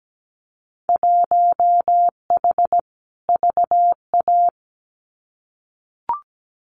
Morse code, heard as a keyed tone, about 17 wpm, 700 Hz. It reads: 1H VA
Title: Winter Field Day Exchanges 17wpm